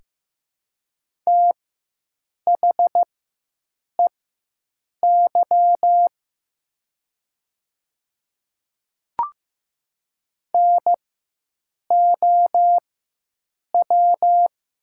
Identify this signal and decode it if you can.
Morse code, heard as a keyed tone, about 15 wpm, spaced as for 7 wpm, 700 Hz. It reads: THEY NOW
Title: 4x Character Spacing - Top 100 Words - 15wpm